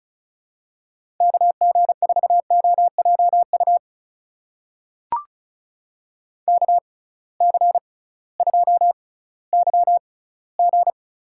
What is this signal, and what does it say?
Morse code, heard as a keyed tone, about 34 wpm, spaced as for 19 wpm, 700 Hz. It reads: KG4OJU KC2YG
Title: USA Call Signs - 6 Characters - 6x Character Spacing - 35wpm